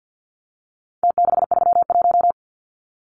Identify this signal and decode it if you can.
Morse code, heard as a keyed tone, about 50 wpm, 700 Hz. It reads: T631
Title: Numbers - Contest Style - 4-Digit 50wpm